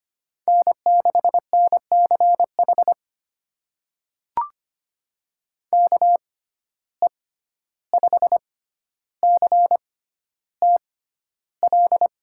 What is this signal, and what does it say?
Morse code, heard as a keyed tone, about 25 wpm, spaced as for 12 wpm, 700 Hz. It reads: N6NCH KE5CTL
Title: USA Call Signs - 6x Character Spacing - 25wpm